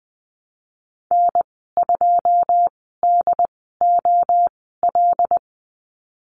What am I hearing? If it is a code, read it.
Morse code, heard as a keyed tone, about 20 wpm, spaced as for 15 wpm, 700 Hz. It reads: N2DOL
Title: 088 Lesson 15g - Number 5 - Call Signs 20wpm